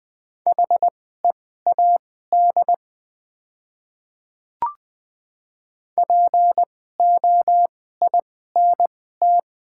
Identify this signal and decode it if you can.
Morse code, heard as a keyed tone, about 20 wpm, spaced as for 14 wpm, 700 Hz. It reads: HEAD POINT
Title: 2x Character Spacing - Top 300 Words - 20wpm